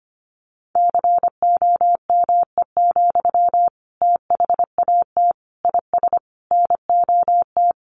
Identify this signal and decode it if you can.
Morse code, heard as a keyed tone, about 25 wpm, 700 Hz. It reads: COME, THAT IS NOT
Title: Sentences from Top 700 Words 25wpm